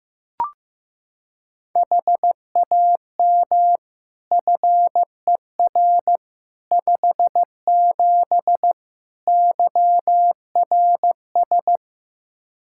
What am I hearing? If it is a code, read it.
Morse code, heard as a keyed tone, about 15 wpm, 700 Hz. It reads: HAM FER 57 YRS
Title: K1USN SST QSO Practice 15wpm